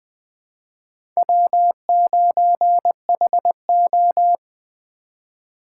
Morse code, heard as a keyed tone, about 20 wpm, 700 Hz. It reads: W9HO
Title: USA Call Signs - 4 Characters - 2x Character Spacing - 20wpm